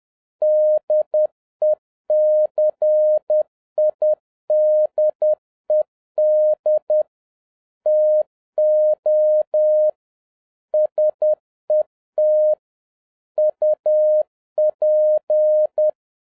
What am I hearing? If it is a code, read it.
Morse code, heard as a keyed tone, about 10 wpm, 610 Hz. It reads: DECIDED TO SET UP
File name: 2011-07-mp3 / n41 EXCITING QRP CW OPERATION 10wpm